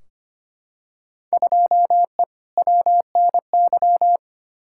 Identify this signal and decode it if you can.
Morse code, heard as a keyed tone, about 25 wpm, 700 Hz. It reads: 2E WNY
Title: Field Day Exchanges Encore - 25wpm